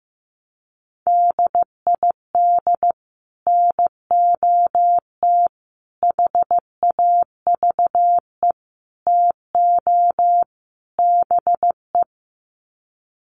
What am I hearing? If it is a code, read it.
Morse code, heard as a keyed tone, about 15 wpm, 700 Hz. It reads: DID NOT HAVE TO BE
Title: Sets of 5 Words Encore2 15wpm